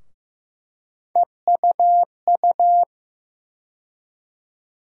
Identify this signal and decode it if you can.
Morse code, heard as a keyed tone, about 15 wpm, 700 Hz. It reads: EUU
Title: USA Call Sign Postfixes - 3 Letters 15wpm